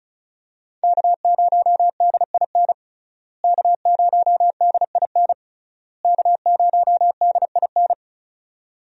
Morse code, heard as a keyed tone, about 36 wpm, 700 Hz. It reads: K0DIN K0DIN K0DIN
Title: USA Call Signs - 5 Characters - Repeated 3x - 35wpm